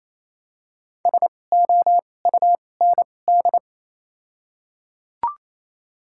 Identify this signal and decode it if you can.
Morse code, heard as a keyed tone, about 28 wpm, spaced as for 20 wpm, 700 Hz. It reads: SOUND
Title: 2x Character Spacing - Top 500 Words - 28wpm